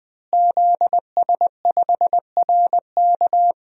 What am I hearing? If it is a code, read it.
Morse code, heard as a keyed tone, about 20 wpm, 700 Hz. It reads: ZS5RK
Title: DXCC-100 Call Signs - 5 Characters - Repeated 2x - 20wpm